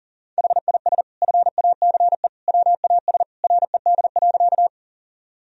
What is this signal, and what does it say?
Morse code, heard as a keyed tone, about 40 wpm, 700 Hz. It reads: HIS FACE WAS RED.
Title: Sentences from Top 1000 Words - Review Part 2 - 40wpm